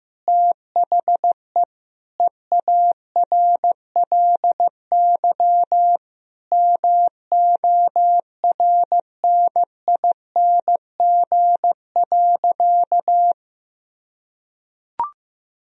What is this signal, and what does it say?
Morse code, heard as a keyed tone, about 15 wpm, 700 Hz. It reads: THE EARLY MORNING.
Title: Sentences from Top 1000 Words - Review Part 3 - 15wpm